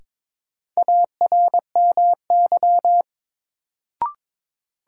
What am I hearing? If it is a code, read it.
Morse code, heard as a keyed tone, about 22 wpm, 700 Hz. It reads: ARMY
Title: Sentences from Top 800 Words - New Word Drill - 22wpm